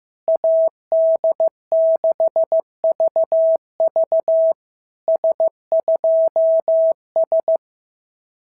Morse code, heard as a keyed tone, about 15 wpm, 650 Hz. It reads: AD6VV S2S
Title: SOTA QSO Practice 15wpm